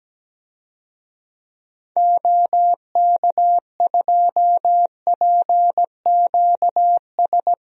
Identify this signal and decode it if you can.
Morse code, heard as a keyed tone, about 17 wpm, 700 Hz. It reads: OK2PQS
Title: DXCC-100 Call Signs - 6 Characters - 17wpm